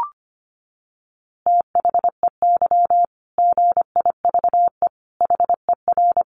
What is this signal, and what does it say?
Morse code, heard as a keyed tone, about 25 wpm, 700 Hz. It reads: THEY GIVE HER
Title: Easy Phrases 25wpm